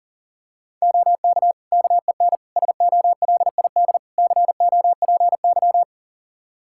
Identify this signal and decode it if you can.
Morse code, heard as a keyed tone, about 40 wpm, 700 Hz. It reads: OK KEN SOLID COPY
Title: Practice QSO - Exchange 3 - More Info - 40wpm